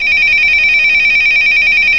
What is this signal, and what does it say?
There is no code.